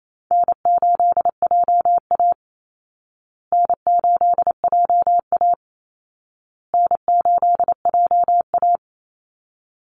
Morse code, heard as a keyed tone, about 28 wpm, 700 Hz. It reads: N8JA N8JA N8JA
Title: Easy Call Signs Repeated 3x - 28wpm